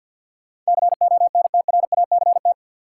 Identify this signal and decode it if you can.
Morse code, heard as a keyed tone, about 52 wpm, 700 Hz. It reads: CONTRACT
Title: Sentences from Top 1400 Words - New Word Drill - 50wpm